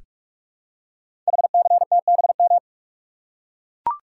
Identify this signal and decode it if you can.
Morse code, heard as a keyed tone, about 44 wpm, 700 Hz. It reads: HCTBM
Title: Code Groups - 5 Letters - 45wpm